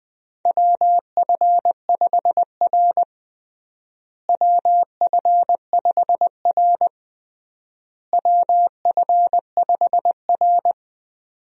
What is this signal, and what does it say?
Morse code, heard as a keyed tone, about 21 wpm, 690 Hz. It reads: WF5R WF5R WF5R
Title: USA Call Signs - 4 Characters - Repeated 3x - 20wpm